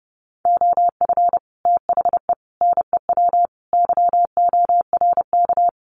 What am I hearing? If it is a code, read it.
Morse code, heard as a keyed tone, about 30 wpm, 700 Hz. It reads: OF THE NEW YORK
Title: Sets of 4 Words 30wpm